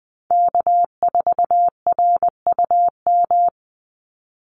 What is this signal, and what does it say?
Morse code, heard as a keyed tone, about 20 wpm, 700 Hz. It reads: K4RUM
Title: Call Signs Repeated 3x - 20wpm